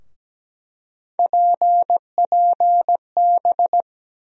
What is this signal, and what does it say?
Morse code, heard as a keyed tone, about 17 wpm, 700 Hz. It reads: PPB
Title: USA Call Sign Postfixes - 3 Letters 17wpm